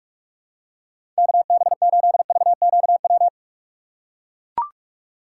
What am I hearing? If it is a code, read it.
Morse code, heard as a keyed tone, about 43 wpm, 690 Hz. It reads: KB8VQW
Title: USA Call Signs - 4x Character Spacing - 45wpm